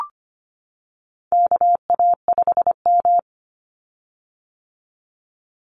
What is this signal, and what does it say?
Morse code, heard as a keyed tone, about 25 wpm, 700 Hz. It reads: KA5M
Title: Call Signs - 25wpm